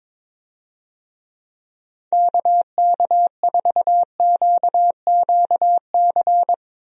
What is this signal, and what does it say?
Morse code, heard as a keyed tone, about 22 wpm, 700 Hz. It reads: KK4QQC
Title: USA Call Signs - 6 Characters - Repeated 3x - 22wpm